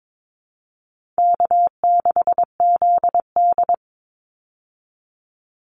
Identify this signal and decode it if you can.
Morse code, heard as a keyed tone, about 22 wpm, 700 Hz. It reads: K6ZD
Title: Call Signs Repeated 2x - 22wpm